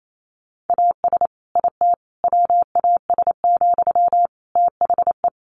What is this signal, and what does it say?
Morse code, heard as a keyed tone, about 28 wpm, 700 Hz. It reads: AS IT WAS, THE
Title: Sentences from Top 1500 Words 28wpm